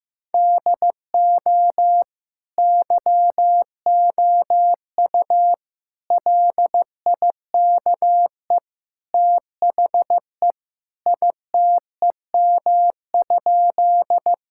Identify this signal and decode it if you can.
Morse code, heard as a keyed tone, about 16 wpm, 700 Hz. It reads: DO YOU LIKE THE ITEM?
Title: Sentences from Top 1000 Words - Review Part 6 - 15wpm